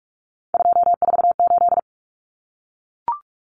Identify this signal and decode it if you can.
Morse code, heard as a keyed tone, about 44 wpm, 700 Hz. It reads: 248
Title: Numbers - 3-Digit 45wpm